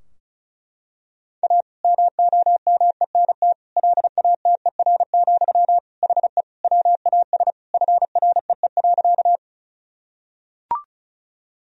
Morse code, heard as a keyed tone, about 36 wpm, 700 Hz. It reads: A MOMENT LATER, HE WAS FREE.
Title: Sentences from Top 1000 Words - Review Part 4 - 35wpm